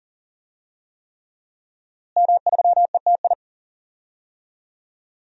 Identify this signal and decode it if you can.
Morse code, heard as a keyed tone, about 40 wpm, 700 Hz. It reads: M3ETI
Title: DXCC-100 Call Signs - 5 Characters - Repeated 3x - 40wpm